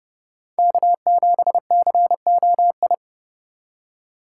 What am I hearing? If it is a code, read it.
Morse code, heard as a keyed tone, about 30 wpm, 700 Hz. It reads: K7COI
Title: USA Call Signs - 3x Character Spacing - 30wpm